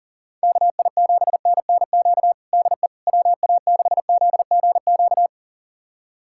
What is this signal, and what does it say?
Morse code, heard as a keyed tone, about 40 wpm, 700 Hz. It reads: KI7NNQ DE WA6ZGQ
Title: USA Callsign DE Callsign - 6 Characters - 40wpm